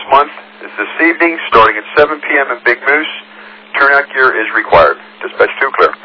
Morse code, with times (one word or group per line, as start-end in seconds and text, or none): none